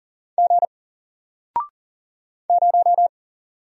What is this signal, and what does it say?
Morse code, heard as a keyed tone, about 40 wpm, 700 Hz. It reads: G 0
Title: Single Letter-Number 40wpm